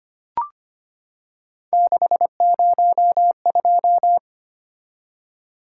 Morse code, heard as a keyed tone, about 25 wpm, 700 Hz. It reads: 602
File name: Three Letter-Number Combinations - 25wpm